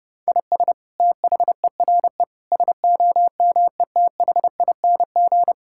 Morse code, heard as a keyed tone, about 30 wpm, 700 Hz. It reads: IS THERE SOMETHING
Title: Sentences from Top 1000 Words - Review Part 6 - 30wpm